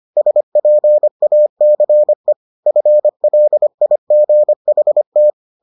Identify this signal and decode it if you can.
Morse code, heard as a keyed tone, about 25 wpm, 600 Hz. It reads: SPACE FLIGHT